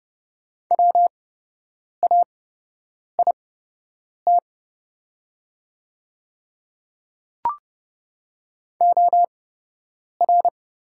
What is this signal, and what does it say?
Morse code, heard as a keyed tone, about 30 wpm, spaced as for 8 wpm, 700 Hz. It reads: WAIT OR
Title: 8x Character Spacing - Top 500 Words - 30wpm